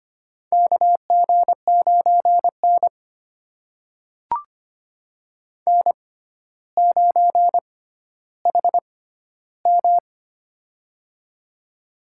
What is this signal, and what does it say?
Morse code, heard as a keyed tone, about 25 wpm, spaced as for 13 wpm, 700 Hz. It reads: KG9N N9HM
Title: USA Call Signs - 6x Character Spacing - 25wpm